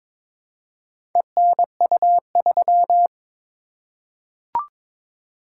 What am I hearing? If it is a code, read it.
Morse code, heard as a keyed tone, about 22 wpm, 700 Hz. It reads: ENU3
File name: practice / 22wpm / Four Letter-Number Combinations - 22wpm